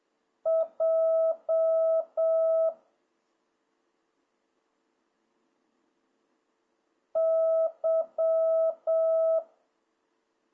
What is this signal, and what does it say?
Morse code, heard as a keyed tone, about 7 wpm, 650 Hz. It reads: J Y